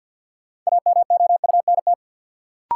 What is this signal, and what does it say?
Morse code, heard as a keyed tone, about 52 wpm, 700 Hz. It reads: AMOUNT
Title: Sets of 3 Words - New Word Drill - 50wpm